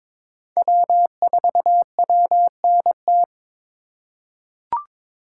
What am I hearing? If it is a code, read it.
Morse code, heard as a keyed tone, about 22 wpm, 700 Hz. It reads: W4WNT
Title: K1USN SST QSO Practice - 22wpm